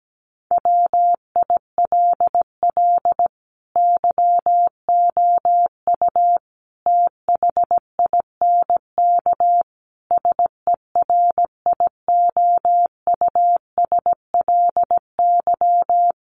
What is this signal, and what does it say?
Morse code, heard as a keyed tone, about 17 wpm, 700 Hz. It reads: WILL YOU THINK SERIOUSLY
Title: Sentences from Top 1200 Words 17wpm